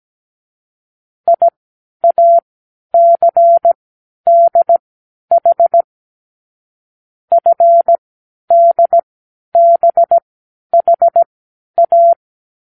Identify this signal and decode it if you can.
Morse code, heard as a keyed tone, about 17 wpm, spaced as for 10 wpm, 680 Hz. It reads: IACDH FDBHA
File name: k7qo / Lesson15(A-J)